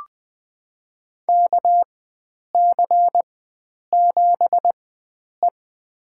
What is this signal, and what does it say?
Morse code, heard as a keyed tone, about 21 wpm, spaced as for 9 wpm, 700 Hz. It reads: KC7E
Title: USA Call Signs - 4 Characters - 4x Character Spacing - 20wpm